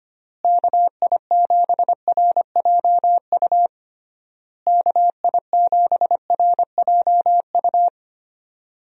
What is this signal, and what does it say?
Morse code, heard as a keyed tone, about 25 wpm, 700 Hz. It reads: KI7RJU KI7RJU
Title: USA Call Signs - 6 Characters - Repeated 3x - 25wpm